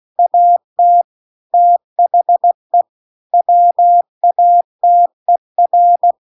Morse code, heard as a keyed tone, about 16 wpm, 700 Hz. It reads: AT THE WATER
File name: CW-The-Big-Twoheaded-River-2-16WPM